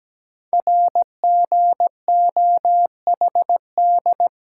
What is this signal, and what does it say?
Morse code, heard as a keyed tone, about 17 wpm, 700 Hz. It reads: RGOHD
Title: Code Groups - 5 Letters - 17wpm